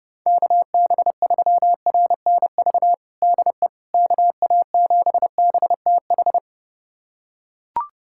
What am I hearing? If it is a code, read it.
Morse code, heard as a keyed tone, about 30 wpm, 690 Hz. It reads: KB3RNV DE KA7BTH
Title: USA Callsign DE Callsign 30wpm